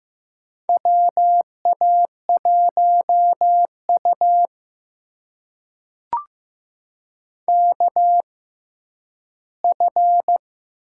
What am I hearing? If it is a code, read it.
Morse code, heard as a keyed tone, about 15 wpm, spaced as for 7 wpm, 700 Hz. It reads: WA1UKF
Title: USA Call Signs - 6x Character Spacing - 15wpm